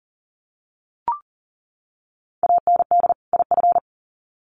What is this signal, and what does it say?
Morse code, heard as a keyed tone, about 40 wpm, 700 Hz. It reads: AND IF
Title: Sets of 2 Words 40wpm